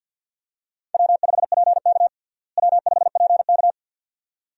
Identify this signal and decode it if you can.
Morse code, heard as a keyed tone, about 50 wpm, 700 Hz. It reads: W5PK W5PK
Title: USA Call Signs - 4 Characters - Repeated 2x - 50wpm